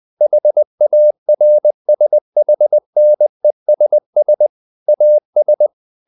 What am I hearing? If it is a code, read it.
Morse code, heard as a keyed tone, about 20 wpm, 600 Hz. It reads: HARSHNESS AS
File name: the old man and the sea / the old man and the sea part 1 20wpm